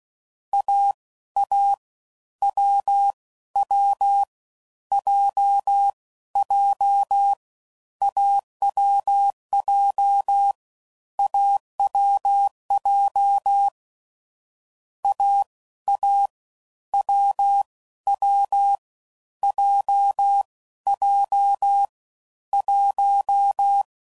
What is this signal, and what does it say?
Morse code, heard as a keyed tone, about 16 wpm, 800 Hz. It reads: A A W W J J AWJ AWJ A A W W J J 1
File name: Combinations / Difficult Combos / awj1 ndb6 gz7